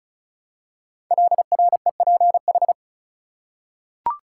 Morse code, heard as a keyed tone, about 35 wpm, 700 Hz. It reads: LREPH
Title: Code Groups - 5 Letters - 35wpm